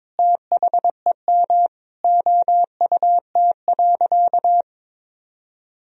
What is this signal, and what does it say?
Morse code, heard as a keyed tone, about 21 wpm, 700 Hz. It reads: THEM OUT.
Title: Sentences from Top 1000 Words - Review Part 1 - 22wpm